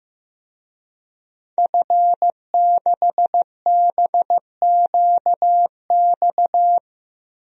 Morse code, heard as a keyed tone, about 15 wpm, 700 Hz. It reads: F6BQX
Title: DXCC-100 Call Signs - 5 Characters - 15wpm